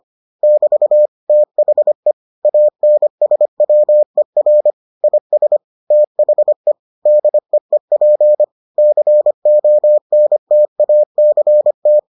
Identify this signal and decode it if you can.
Morse code, heard as a keyed tone, about 25 wpm, 600 Hz. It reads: <BT> THE ANSWER IS THE DEEP CONTACT